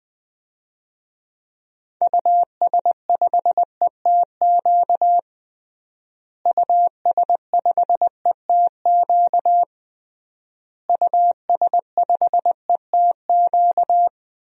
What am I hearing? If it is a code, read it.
Morse code, heard as a keyed tone, about 20 wpm, 700 Hz. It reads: US5ETQ US5ETQ US5ETQ
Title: DXCC-100 Call Signs - 6 Characters - Repeated 3x - 20wpm